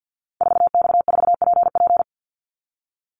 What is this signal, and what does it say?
Morse code, heard as a keyed tone, about 50 wpm, 700 Hz. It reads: 4X4PP
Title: Call Signs - 50wpm